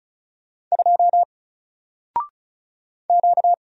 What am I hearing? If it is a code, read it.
Morse code, heard as a keyed tone, about 36 wpm, 700 Hz. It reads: 2 Q
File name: Single-Letter-Number-35wpm